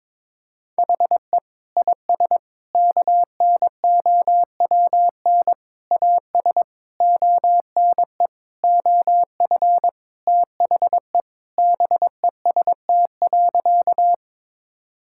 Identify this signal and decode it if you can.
Morse code, heard as a keyed tone, about 21 wpm, 700 Hz. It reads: HE IS KNOWN AS ONE OF THE BEST.
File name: practice / 22wpm / Sentences with 4-Word Phrases - 22wpm